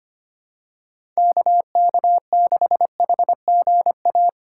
Morse code, heard as a keyed tone, about 25 wpm, 700 Hz. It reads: KK6HGA
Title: USA Call Signs - 6 Characters - 8x Character Spacing - 25wpm